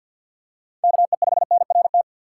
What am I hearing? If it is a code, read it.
Morse code, heard as a keyed tone, about 53 wpm, 700 Hz. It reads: KE5NRT
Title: USA Call Signs - 6 Characters - 3x Character Spacing - 50wpm